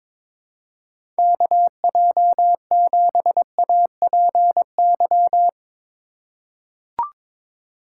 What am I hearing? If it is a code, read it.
Morse code, heard as a keyed tone, about 22 wpm, 700 Hz. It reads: KJ7APY
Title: USA Call Signs - 6 Characters - 3x Character Spacing - 22wpm